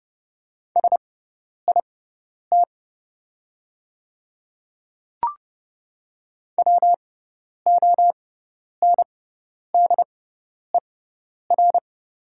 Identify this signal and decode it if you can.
Morse code, heard as a keyed tone, about 30 wpm, spaced as for 10 wpm, 690 Hz. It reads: SIT WONDER
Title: 6x Character Spacing - Top 500 Words - 30wpm